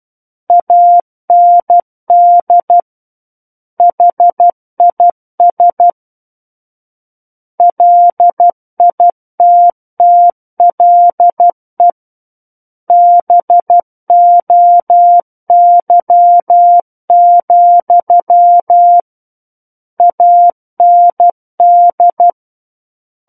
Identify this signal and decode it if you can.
Morse code, heard as a keyed tone, about 12 wpm, 700 Hz. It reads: AND HIS LITTLE BOY, AND
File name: Book One / 03-Chapter 3 - 12 WPM